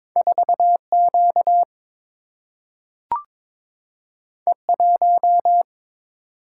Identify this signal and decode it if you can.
Morse code, heard as a keyed tone, about 22 wpm, 700 Hz. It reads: 4 Q E 1
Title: Single-Letter Single-Number 22wpm